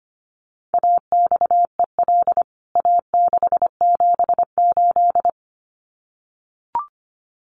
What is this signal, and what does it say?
Morse code, heard as a keyed tone, about 25 wpm, 700 Hz. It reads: AXEL A678
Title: CWT Exchange 25wpm